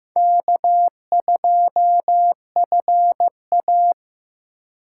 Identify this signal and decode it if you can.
Morse code, heard as a keyed tone, about 15 wpm, 700 Hz. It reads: K2FA
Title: USA Call Signs - 4 Characters - 4x Character Spacing - 15wpm